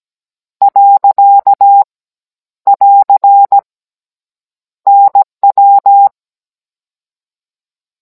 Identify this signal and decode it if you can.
Morse code, heard as a keyed tone, about 17 wpm, 800 Hz. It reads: . <AR> NW